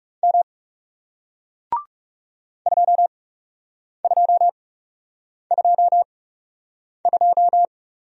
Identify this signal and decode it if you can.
Morse code, heard as a keyed tone, about 39 wpm, spaced as for 8 wpm, 700 Hz. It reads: M 2222